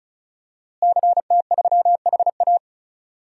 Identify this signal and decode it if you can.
Morse code, heard as a keyed tone, about 34 wpm, 700 Hz. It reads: CT3HA
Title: DXCC-100 Call Signs - 5 Characters - Repeated 2x - 35wpm